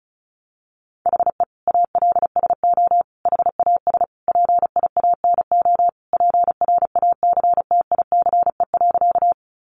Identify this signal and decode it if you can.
Morse code, heard as a keyed tone, about 34 wpm, 700 Hz. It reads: HE ALSO HAS PIANO PRACTICE.
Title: Additional Easy Phrases 35wpm